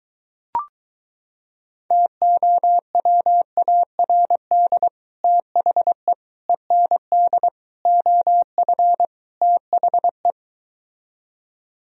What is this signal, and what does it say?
Morse code, heard as a keyed tone, about 22 wpm, 700 Hz. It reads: TOWARD THE END OF THE